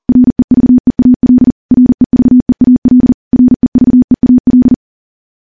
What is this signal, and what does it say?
Morse code, heard as a keyed tone, about 40 wpm, 250 Hz. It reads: REVEAL REVEAL REVEAL
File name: morse / morse-training-40WPM